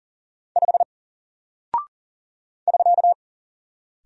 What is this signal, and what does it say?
Morse code, heard as a keyed tone, about 40 wpm, 700 Hz. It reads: 5 <SK>
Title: Single Character (Letter:Number:Punctuation:Prosign) Alt 40wpm